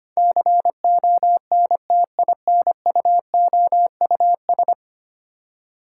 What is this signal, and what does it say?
Morse code, heard as a keyed tone, about 25 wpm, 700 Hz. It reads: CONTINUOUS
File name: Ten Letter Words - 25wpm